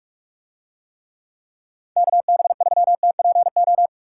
Morse code, heard as a keyed tone, about 44 wpm, 700 Hz. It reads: KB3TPO